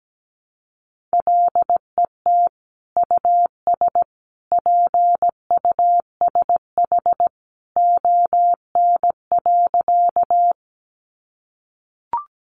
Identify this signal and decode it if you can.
Morse code, heard as a keyed tone, about 17 wpm, 700 Hz. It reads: LET US PUSH ON.
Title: Sentences from Top 1300 Words 17wpm